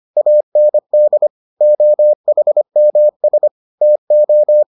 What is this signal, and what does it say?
Morse code, heard as a keyed tone, about 25 wpm, 600 Hz. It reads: AND OHMS TO